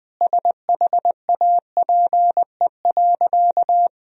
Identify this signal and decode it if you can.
Morse code, heard as a keyed tone, about 20 wpm, 700 Hz. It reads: SHAPE.
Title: Sentences from Top 1000 Words - Review Part 3 - 20wpm